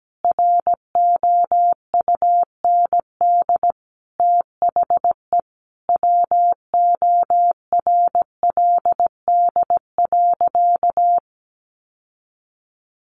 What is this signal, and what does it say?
Morse code, heard as a keyed tone, about 17 wpm, 700 Hz. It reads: ROUND THE WORLD.
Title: Sentences from Top 600 Words 17wpm